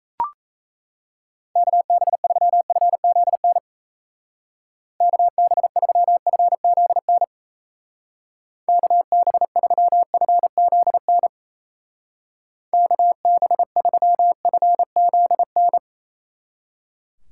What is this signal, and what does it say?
Morse code, heard as a keyed tone, about 34 wpm, 700 Hz. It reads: KB3FZN KB3FZN KB3FZN KB3FZN
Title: Speed Racing - USA 2x3 Call Signs 28wpm